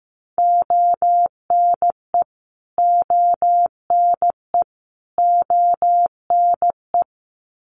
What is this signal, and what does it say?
Morse code, heard as a keyed tone, about 15 wpm, 700 Hz. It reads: ONE ONE ONE
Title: Top 500 Words Repeated 15wpm